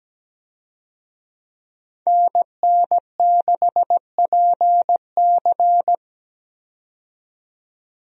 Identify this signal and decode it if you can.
Morse code, heard as a keyed tone, about 17 wpm, 700 Hz. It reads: NN6PC
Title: USA Call Signs - 5 Characters - 17wpm